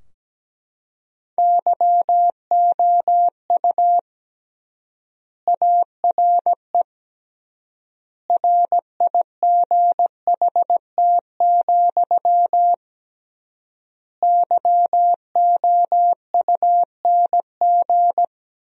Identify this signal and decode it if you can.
Morse code, heard as a keyed tone, about 17 wpm, 700 Hz. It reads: YOU ARE RIGHT, YOUNG
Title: 3x Word-Spacing - Sentences from Top 400 Words - 17wpm